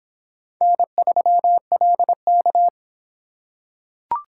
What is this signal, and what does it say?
Morse code, heard as a keyed tone, about 25 wpm, 700 Hz. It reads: N3LK